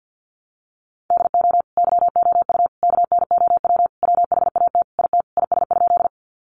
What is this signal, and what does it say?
Morse code, heard as a keyed tone, about 50 wpm, 700 Hz. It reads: DO YOU KNOW WHAT IT IS?
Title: Sentences from Top 100 Words 50wpm